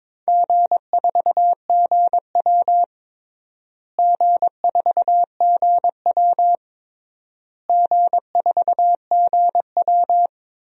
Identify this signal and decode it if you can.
Morse code, heard as a keyed tone, about 22 wpm, 700 Hz. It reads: G4GW G4GW G4GW
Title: DXCC-100 Call Signs - 4 Characters - Repeated 3x - 22wpm